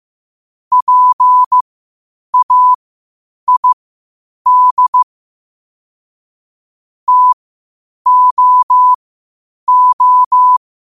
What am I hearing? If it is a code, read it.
Morse code, heard as a keyed tone, about 15 wpm, spaced as for 8 wpm, 1000 Hz. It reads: PAID TOO
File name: qotdfiles / 8wpm